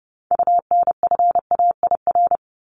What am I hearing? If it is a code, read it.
Morse code, heard as a keyed tone, about 30 wpm, 700 Hz. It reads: UNFAIR
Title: Top 1751-2000 Words 30wpm